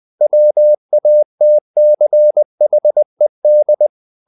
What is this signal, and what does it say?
Morse code, heard as a keyed tone, about 20 wpm, 600 Hz. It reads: WATCHED
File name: the old man and the sea part 2 20wpm